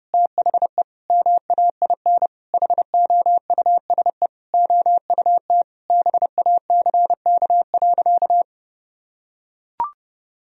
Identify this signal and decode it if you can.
Morse code, heard as a keyed tone, about 30 wpm, 700 Hz. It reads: THE MAIN HOUSE OUT BACK.
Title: Sentences from Top 1000 Words - Review Part 2 - 30wpm